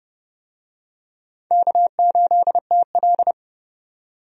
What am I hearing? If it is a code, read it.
Morse code, heard as a keyed tone, about 30 wpm, 700 Hz. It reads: K8TL
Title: USA Call Signs - 4 Characters - 30wpm